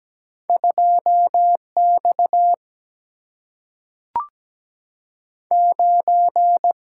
Morse code, heard as a keyed tone, about 17 wpm, 700 Hz. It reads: 2X 9
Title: Single-Letter Single-Number 17wpm